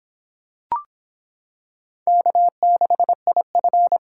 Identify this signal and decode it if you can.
Morse code, heard as a keyed tone, about 25 wpm, 700 Hz. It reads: K6IF